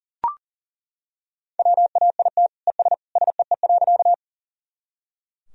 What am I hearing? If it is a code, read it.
Morse code, heard as a keyed tone, about 40 wpm, 700 Hz. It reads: WAIT ES SEE.
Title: Sentences from Top 500 Words and Abbreviations 40wpm